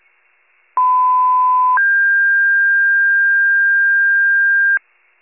Two-tone page, heard as a one-second tone, then a three-second tone, about 1010 then 1600 Hz. It is heard from 0.8 to 4.8 s.
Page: County Wide Paging Archive